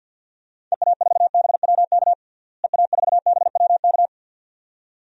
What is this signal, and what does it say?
Morse code, heard as a keyed tone, about 45 wpm, 700 Hz. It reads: EA4BWK EA4BWK
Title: DXCC-100 Call Signs - 6 Characters - Repeated 2x - 50wpm